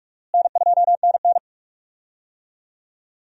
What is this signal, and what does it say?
Morse code, heard as a keyed tone, about 46 wpm, 700 Hz. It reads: N2NN
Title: USA Call Signs - 4 Characters - Repeated 3x - 45wpm